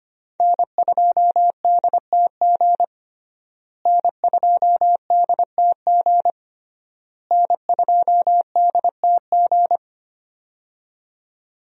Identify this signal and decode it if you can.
Morse code, heard as a keyed tone, about 25 wpm, 700 Hz. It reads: N2DTG N2DTG N2DTG
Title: USA Call Signs - 5 Characters - Repeated 3x - 25wpm